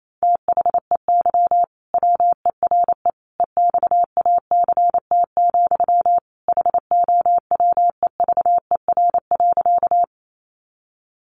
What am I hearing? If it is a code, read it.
Morse code, heard as a keyed tone, about 28 wpm, 700 Hz. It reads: THEY WERE EXACT, HOWEVER.
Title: Sentences from Top 1300 Words 28wpm